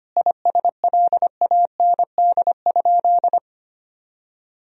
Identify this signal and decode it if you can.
Morse code, heard as a keyed tone, about 25 wpm, 700 Hz. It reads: ISLAND?
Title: Sentences from Top 1000 Words - Review Part 3 - 25wpm